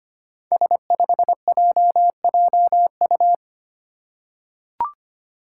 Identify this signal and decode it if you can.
Morse code, heard as a keyed tone, about 25 wpm, 700 Hz. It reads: S5JJU
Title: Five Challenging Letter-Number Combinations 25wpm